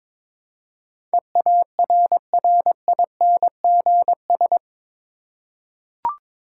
Morse code, heard as a keyed tone, about 22 wpm, 700 Hz. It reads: EARRINGS
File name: practice / 22wpm / Difficult Words with Hints - 22wpm